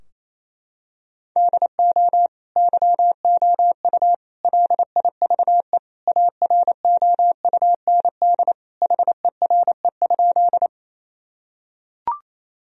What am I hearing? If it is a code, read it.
Morse code, heard as a keyed tone, about 28 wpm, 700 Hz. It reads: DO YOU LIVE AROUND HERE?
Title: Sentences from Top 1000 Words - Review Part 3 - 28wpm